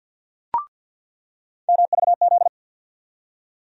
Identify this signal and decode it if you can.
Morse code, heard as a keyed tone, about 47 wpm, 700 Hz. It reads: MVZ
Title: USA Call Sign Postfixes - 3 Letters 50wpm